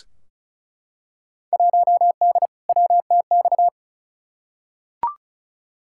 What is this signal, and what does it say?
Morse code, heard as a keyed tone, about 34 wpm, 700 Hz. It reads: 1D WTX
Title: Field Day Exchanges - 35wpm